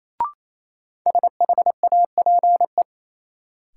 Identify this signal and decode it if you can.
Morse code, heard as a keyed tone, about 28 wpm, 700 Hz. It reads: SHAPE
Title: Sentences from Top 500 Words - New Word Drill - 28wpm